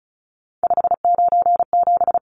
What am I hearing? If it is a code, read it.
Morse code, heard as a keyed tone, about 34 wpm, 700 Hz. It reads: 597
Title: Numbers - 3-Digit 35wpm